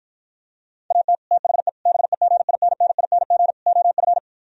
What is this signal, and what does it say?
Morse code, heard as a keyed tone, about 52 wpm, 700 Hz. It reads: AT THE BEGINNING OF